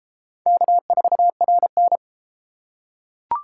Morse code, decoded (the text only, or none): K4RN